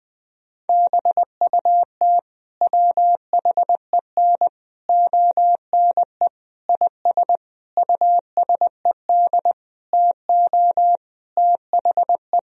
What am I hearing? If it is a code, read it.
Morse code, heard as a keyed tone, about 20 wpm, 700 Hz. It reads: BUT WHEN ONE IS USED TO THE
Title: Sentences from Top 1000 Words - Review Part 1 - 20wpm